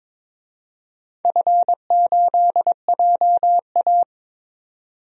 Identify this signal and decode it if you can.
Morse code, heard as a keyed tone, about 22 wpm, 700 Hz. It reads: F8JA
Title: DXCC-100 Call Signs - 4 Characters - 22wpm